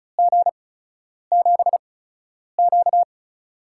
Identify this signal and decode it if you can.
Morse code, heard as a keyed tone, about 36 wpm, spaced as for 10 wpm, 700 Hz. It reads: G7Q
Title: USA Call Signs - 4 Characters - 8x Character Spacing - 35wpm